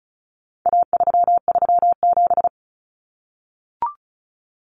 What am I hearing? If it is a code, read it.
Morse code, heard as a keyed tone, about 34 wpm, 700 Hz. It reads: A337
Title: Common Cut Numbers - 4-Digit 35wpm